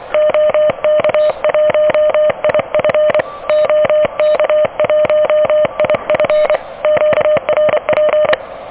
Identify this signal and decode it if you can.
Morse code, heard as a keyed tone, about 25 wpm, 610 Hz. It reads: OK1IF OK1IF QRP
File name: OK QRP 2023 / OK QRP 2023 OK1DWF